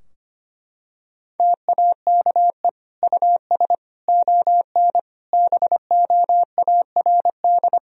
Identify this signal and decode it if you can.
Morse code, heard as a keyed tone, about 25 wpm, 700 Hz. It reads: TAKE US ON BOARD
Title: Sentences from Top 1000 Words - Review Part 4 - 25wpm